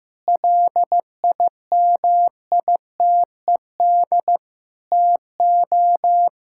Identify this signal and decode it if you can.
Morse code, heard as a keyed tone, about 16 wpm, 700 Hz. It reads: LIMITED TO
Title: Sentences from Top 1000 Words - Review Part 6 - 15wpm